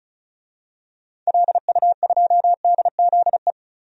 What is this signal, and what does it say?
Morse code, heard as a keyed tone, about 34 wpm, 700 Hz. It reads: LU2DZE
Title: DXCC-100 Call Signs - 6 Characters - 35wpm